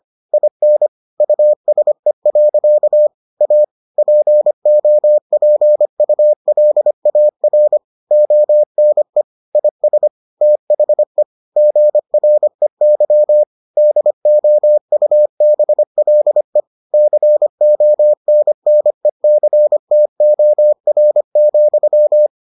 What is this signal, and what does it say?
Morse code, heard as a keyed tone, about 25 wpm, 600 Hz. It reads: IN USE. A POPULAR ONE IS THE GREY DOUBLE CONNECTOR,